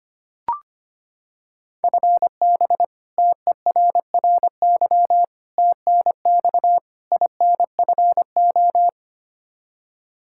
Morse code, heard as a keyed tone, about 25 wpm, 700 Hz. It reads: FB TERRY TNX INFO
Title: Practice QSO - Exchange 3 - More Info - 25wpm